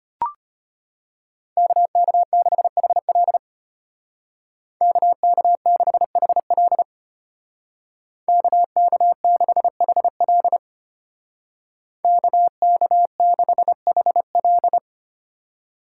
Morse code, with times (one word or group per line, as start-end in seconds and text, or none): 1.6-3.4 KK6HL
4.8-6.8 KK6HL
8.3-10.6 KK6HL
12.0-14.8 KK6HL